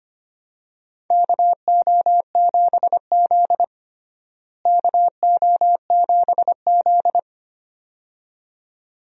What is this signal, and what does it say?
Morse code, heard as a keyed tone, about 25 wpm, 700 Hz. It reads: KO7Z KO7Z
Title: USA Call Signs - 4 Characters - Repeated 2x - 25wpm